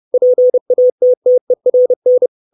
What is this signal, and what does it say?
Morse code, heard as a keyed tone, about 30 wpm, 500 Hz. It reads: PATTERN
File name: 632 Words 30WPM